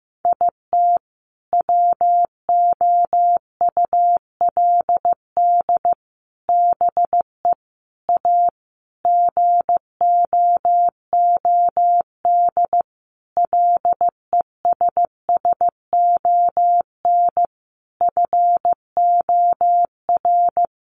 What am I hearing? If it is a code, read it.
Morse code, heard as a keyed tone, about 15 wpm, 700 Hz. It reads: IT WOULD BE A GOOD LESSON FOR
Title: Sentences from Top 1200 Words 15wpm